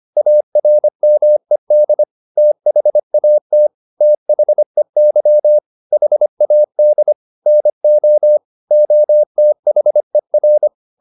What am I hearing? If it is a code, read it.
Morse code, heard as a keyed tone, about 25 wpm, 600 Hz. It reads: ARMED THAT THEY HAD NO OTHER